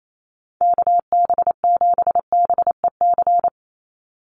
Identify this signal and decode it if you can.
Morse code, heard as a keyed tone, about 28 wpm, 700 Hz. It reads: KB7BEC
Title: USA Call Signs 28wpm